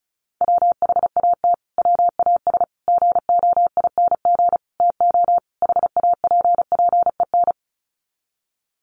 Morse code, heard as a keyed tone, about 34 wpm, 700 Hz. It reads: WHAT WAS GOING TO HAPPEN
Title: Sets of 5 Words Encore 35wpm